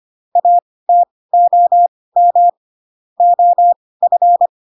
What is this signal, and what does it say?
Morse code, heard as a keyed tone, about 25 wpm, spaced as for 18 wpm, 700 Hz. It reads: ATOM OF
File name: Pinocchio / Pinocchio ch2 12 25 ik1hge